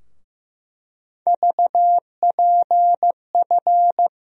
Morse code, heard as a keyed tone, about 15 wpm, 700 Hz. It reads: VPF
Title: USA Call Sign Postfixes - 3 Letters 15wpm